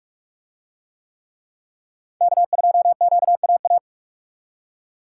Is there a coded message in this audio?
Yes: Morse code from 2.2 to 3.8 s, heard as a keyed tone, about 44 wpm, 700 Hz. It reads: K2QAA